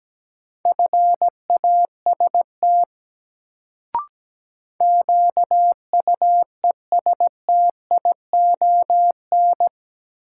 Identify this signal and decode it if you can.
Morse code, heard as a keyed tone, about 17 wpm, 700 Hz. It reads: FAST QUESTION
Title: Sentences from Top 400 Words - New Word Drill - 17wpm